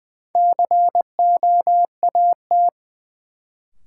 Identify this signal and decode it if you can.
Morse code, heard as a keyed tone, about 20 wpm, 700 Hz. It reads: COAT
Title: Sentences from Top 1600 Words - New Word Drill Alt - 20wpm